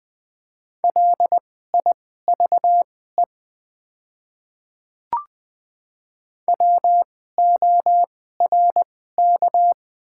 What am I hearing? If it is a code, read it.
Morse code, heard as a keyed tone, about 20 wpm, spaced as for 15 wpm, 690 Hz. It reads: LIVE WORK
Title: 2x Character Spacing - Top 500 Words - 20wpm